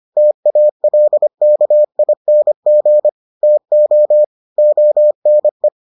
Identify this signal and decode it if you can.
Morse code, heard as a keyed tone, about 25 wpm, 600 Hz. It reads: TALKING TO ONE